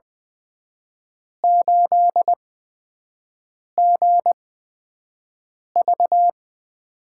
Morse code, heard as a keyed tone, about 21 wpm, spaced as for 6 wpm, 700 Hz. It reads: 8GV
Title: USA Call Signs - 5 Characters - 8x Character Spacing - 20wpm